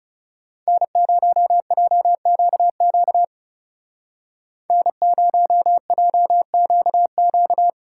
Morse code, heard as a keyed tone, about 36 wpm, 700 Hz. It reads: N0JQQ N0JQQ
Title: Speed Racing - USA 1x3 Call Signs 30wpm